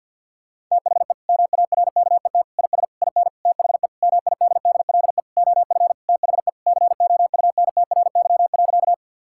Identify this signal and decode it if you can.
Morse code, heard as a keyed tone, about 50 wpm, 700 Hz. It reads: THE MARKET IS IN THE MIDDLE OF THE COUNTRY.
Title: Sentences from Top 1000 Words - Review Part 6 - 50wpm